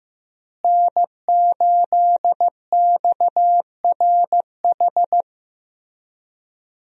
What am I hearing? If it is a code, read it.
Morse code, heard as a keyed tone, about 16 wpm, 700 Hz. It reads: N8XRH
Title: USA Call Signs - 5 Characters - Repeated 3x - 15wpm